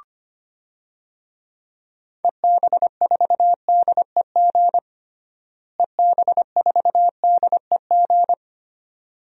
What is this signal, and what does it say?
Morse code, heard as a keyed tone, about 25 wpm, 700 Hz. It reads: EB4DEG EB4DEG
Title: DXCC-100 Call Signs - 6 Characters - Repeated 2x - 25wpm